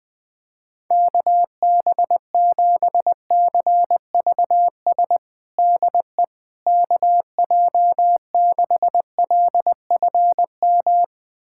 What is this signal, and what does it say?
Morse code, heard as a keyed tone, about 20 wpm, 700 Hz. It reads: KB7CVS DE KJ6LFM
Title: USA Callsign DE Callsign - 6 Characters - 20wpm